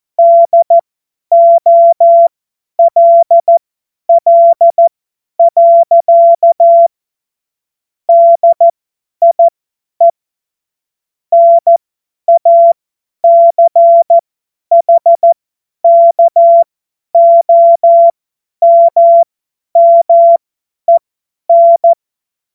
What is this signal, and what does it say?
Morse code, heard as a keyed tone, about 14 wpm, spaced as for 10 wpm, 680 Hz. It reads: DOLL. DIE NACHKOMMEN
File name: Morsekurs / Witze / Witz 06. Januar 2026 50 F70